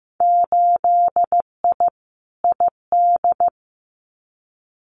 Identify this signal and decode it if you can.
Morse code, heard as a keyed tone, about 15 wpm, 700 Hz. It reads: 8I ID
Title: Winter Field Day Exchanges Encore 15wpm